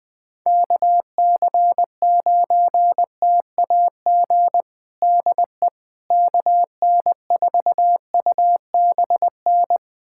Morse code, heard as a keyed tone, about 21 wpm, 700 Hz. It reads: KC9TAG DE KN4UBN
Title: USA Callsign DE Callsign - 6 Characters - 20wpm